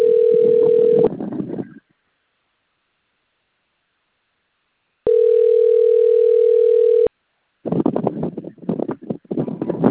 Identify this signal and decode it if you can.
Ringback tone (North America), 440 + 480 Hz.
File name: out- - - - - .82129